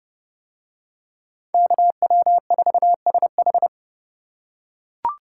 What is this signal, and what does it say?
Morse code, heard as a keyed tone, about 29 wpm, 700 Hz. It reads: KW4SH